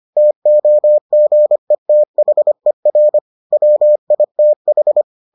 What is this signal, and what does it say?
Morse code, heard as a keyed tone, about 25 wpm, 600 Hz. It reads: TOGETHER WITH